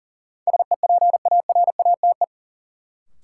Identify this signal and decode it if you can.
Morse code, heard as a keyed tone, about 40 wpm, 700 Hz. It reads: SEPARATE
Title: Sentences from Top 1000 Words - Review Part 7 - New Word Drill - 40wpm